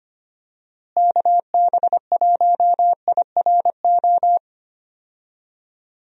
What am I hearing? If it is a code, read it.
Morse code, heard as a keyed tone, about 25 wpm, 700 Hz. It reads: KB1IRO
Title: USA Call Signs - 6 Characters - Repeated 3x - 25wpm